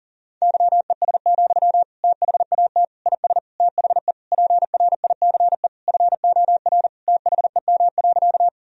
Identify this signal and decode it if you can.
Morse code, heard as a keyed tone, about 40 wpm, 700 Hz. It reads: YES, THAT IS THE PRICE FOR THEM.
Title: Sentences from Top 1000 Words - Review Part 4 - 40wpm